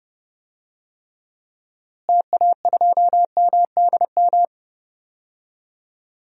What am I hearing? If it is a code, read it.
Morse code, heard as a keyed tone, about 30 wpm, 700 Hz. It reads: TA2MDM
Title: DXCC-100 Call Signs - 6 Characters - Repeated 2x - 30wpm